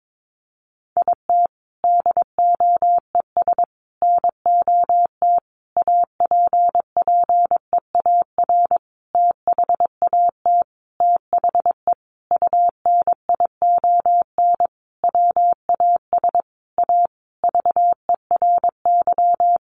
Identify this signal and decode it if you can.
Morse code, heard as a keyed tone, about 22 wpm, 700 Hz. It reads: IT DOES NOT APPEAR THAT THE UNION WAS A VERY
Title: Sentences from Top 1400 Words 22wpm